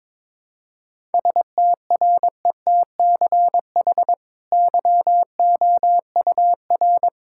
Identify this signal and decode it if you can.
Morse code, heard as a keyed tone, about 22 wpm, 700 Hz. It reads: STRETCH YOUR
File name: practice / 22wpm / Easy Sentences with Hints - 22wpm